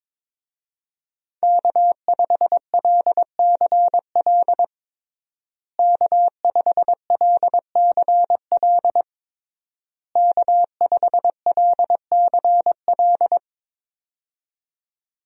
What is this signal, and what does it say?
Morse code, heard as a keyed tone, about 21 wpm, 700 Hz. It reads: K5LCL K5LCL K5LCL
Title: USA Call Signs - 5 Characters - Repeated 3x - 22wpm